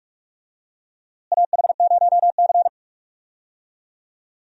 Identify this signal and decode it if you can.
Morse code, heard as a keyed tone, about 42 wpm, 700 Hz. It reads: AH0C
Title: USA Call Signs - 4 Characters - Repeated 2x - 45wpm